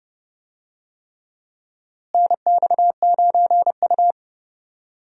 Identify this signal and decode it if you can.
Morse code, heard as a keyed tone, about 31 wpm, 700 Hz. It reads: NX9U
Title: USA Call Signs - 4 Characters - Repeated 3x - 30wpm